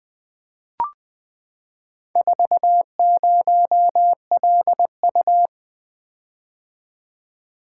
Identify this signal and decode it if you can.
Morse code, heard as a keyed tone, about 20 wpm, 700 Hz. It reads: 40LU
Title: Four Letter-Number Combinations - 20wpm